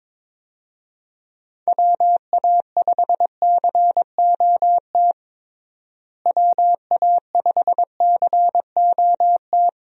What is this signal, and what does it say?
Morse code, heard as a keyed tone, about 22 wpm, 700 Hz. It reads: WA5COT WA5COT
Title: USA Call Signs - 6 Characters - Repeated 3x - 22wpm